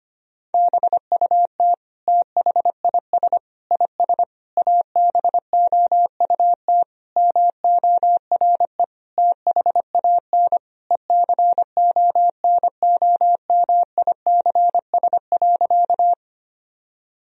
Morse code, heard as a keyed tone, about 25 wpm, 700 Hz. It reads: BUT THIS IS ABOUT MORE THAN ECONOMICS.
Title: Sentences from Top 1000 Words - Review Part 3 - 25wpm